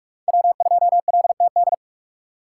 Morse code, heard as a keyed tone, about 43 wpm, 700 Hz. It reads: W2LTD
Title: USA Call Signs - 5 Characters - 2x Character Spacing - 45wpm